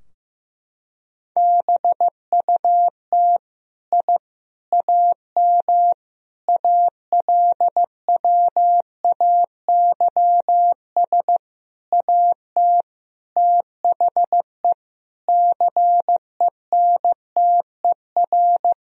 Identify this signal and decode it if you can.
Morse code, heard as a keyed tone, about 16 wpm, 700 Hz. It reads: BUT I AM ALWAYS AT THE CENTER
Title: Sentences from Top 1000 Words - Review Part 2 - 15wpm